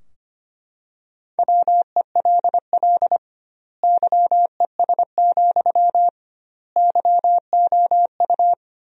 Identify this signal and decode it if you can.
Morse code, heard as a keyed tone, about 25 wpm, 700 Hz. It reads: WELL YES, YOU
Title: 2x Word-Spacing - Sentences from Top 500 Words - 25wpm